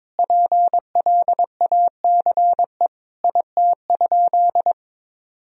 Morse code, heard as a keyed tone, about 22 wpm, 700 Hz. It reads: PLACE IT?
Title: Sentences from Top 200 Words and Abbreviations 22wpm